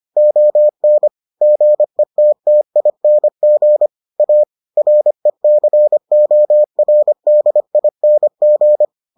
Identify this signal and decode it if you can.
Morse code, heard as a keyed tone, about 25 wpm, 600 Hz. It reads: ON GETTING A RECORDING